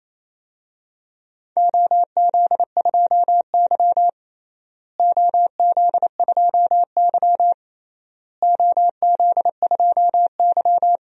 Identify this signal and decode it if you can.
Morse code, heard as a keyed tone, about 28 wpm, 700 Hz. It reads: OZ2Y OZ2Y OZ2Y
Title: DXCC-100 Call Signs - 4 Characters - Repeated 3x - 28wpm